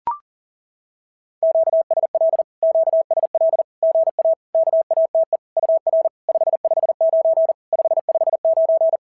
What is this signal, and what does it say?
Morse code, heard as a keyed tone, about 40 wpm, 650 Hz. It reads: QSL QSL GA KATE UR 559 559
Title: SOTA QSO Practice 40wpm